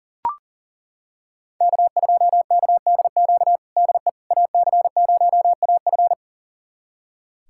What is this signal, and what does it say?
Morse code, heard as a keyed tone, about 40 wpm, 700 Hz. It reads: K2KDQ DE AC0AF
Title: USA Callsign DE Callsign - 5 Characters - 40wpm